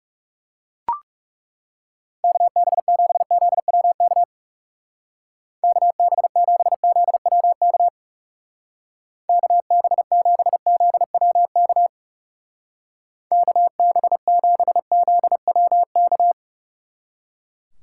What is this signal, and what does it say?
Morse code, heard as a keyed tone, about 36 wpm, 700 Hz. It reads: KB7ZWK KB7ZWK KB7ZWK KB7ZWK
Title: Speed Racing - USA 2x3 Call Signs 30wpm